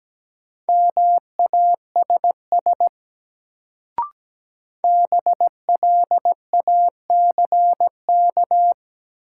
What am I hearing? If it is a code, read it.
Morse code, heard as a keyed tone, about 17 wpm, 700 Hz. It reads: MASS BLACK
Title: Sets of 5 Words Encore - New Word Drill - 17wpm